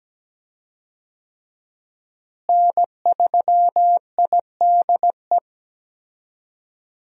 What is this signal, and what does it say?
Morse code, heard as a keyed tone, about 17 wpm, 700 Hz. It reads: N3IDE
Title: USA Call Signs - 5 Characters - Repeated 2x - 17wpm